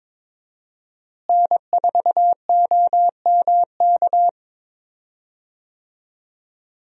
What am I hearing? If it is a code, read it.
Morse code, heard as a keyed tone, about 22 wpm, 700 Hz. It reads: N4OMK
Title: USA Call Signs - 5 Characters - Repeated 3x - 22wpm